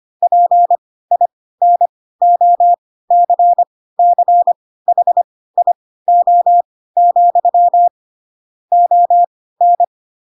Morse code, heard as a keyed tone, about 25 wpm, spaced as for 16 wpm, 700 Hz. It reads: PINOCCHIO, ON
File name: Pinocchio / Pinocchio ch3 10 25 ik1hge